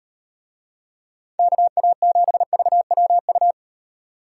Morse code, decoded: KA7VWU

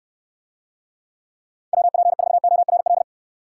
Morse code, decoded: WP4ORL

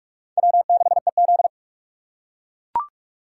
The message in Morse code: W6EZ

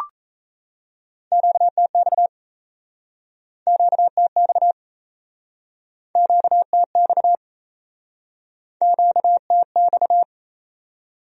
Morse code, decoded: QTX QTX QTX QTX

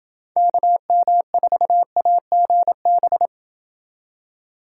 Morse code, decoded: KM4AGB

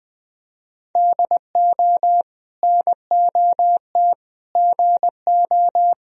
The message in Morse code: DO NOT GO

